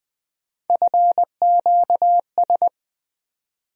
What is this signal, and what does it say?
Morse code, heard as a keyed tone, about 21 wpm, 700 Hz. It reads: FQS
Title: USA Call Sign Postfixes - 3 Letters 20wpm